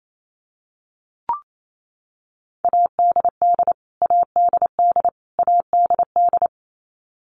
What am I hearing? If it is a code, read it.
Morse code, heard as a keyed tone, about 28 wpm, 700 Hz. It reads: ADD ADD ADD
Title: Top 200 Words Repeated 28wpm